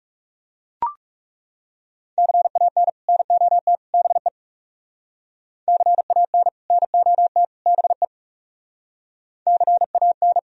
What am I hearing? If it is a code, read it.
Morse code, heard as a keyed tone, about 40 wpm, 700 Hz. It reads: CAN NOT BE CAN NOT BE CAN